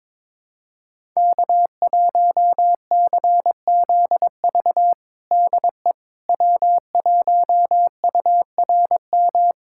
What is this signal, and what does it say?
Morse code, heard as a keyed tone, about 21 wpm, 700 Hz. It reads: K1CZV DE W1URM
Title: USA Callsign DE Callsign - 5 Characters - 22wpm